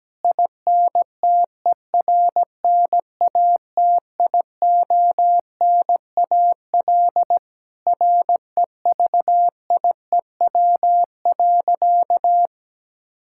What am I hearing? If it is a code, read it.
Morse code, heard as a keyed tone, about 17 wpm, 700 Hz. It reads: INTERNATIONAL REVIEW.
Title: Sentences from Top 1000 Words - Review Part 6 - 17wpm